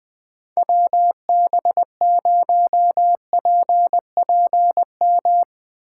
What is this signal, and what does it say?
Morse code, heard as a keyed tone, about 20 wpm, 700 Hz. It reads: WB0PPM
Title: Speed Racing - USA Call Signs 20wpm